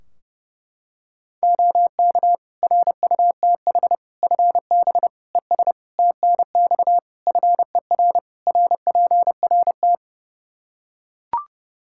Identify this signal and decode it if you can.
Morse code, heard as a keyed tone, about 30 wpm, 700 Hz. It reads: OK RUTH FB ES TNX FER RPRT
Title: Practice QSO - Exchange 2 - Info - 30wpm